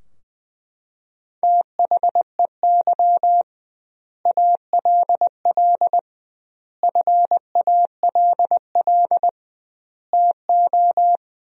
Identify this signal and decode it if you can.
Morse code, heard as a keyed tone, about 21 wpm, 700 Hz. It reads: THEY ALL FALL TO
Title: 2x Word-Spacing - Sentences from Top 500 Words - 20wpm